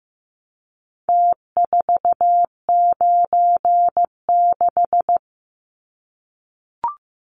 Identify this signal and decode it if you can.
Morse code, heard as a keyed tone, about 15 wpm, 700 Hz. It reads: T496
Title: Numbers - Contest Style - 4-Digit 15wpm